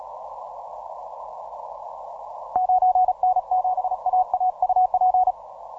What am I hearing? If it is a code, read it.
Morse code, heard as a keyed tone, about 36 wpm, 740 Hz. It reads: 9N7AA UP
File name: simple-file-list / 9N7AA 12MCW